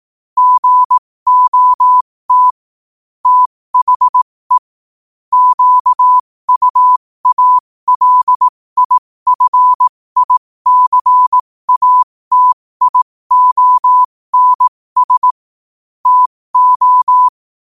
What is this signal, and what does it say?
Morse code, heard as a keyed tone, about 18 wpm, spaced as for 15 wpm, 1000 Hz. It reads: GOT THE QUALIFICATIONS TO
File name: qotdfiles / 15wpm